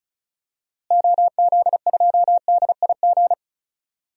Morse code, heard as a keyed tone, about 36 wpm, 700 Hz. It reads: OZ2DIG
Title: DXCC-100 Call Signs - 6 Characters - Repeated 2x - 35wpm